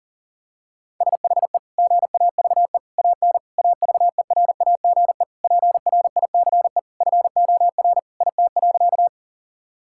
Morse code, heard as a keyed tone, about 40 wpm, 700 Hz. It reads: SHE GAVE AN AVERAGE PRICE FOR IT.
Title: Sentences from Top 1000 Words - Review Part 6 - 40wpm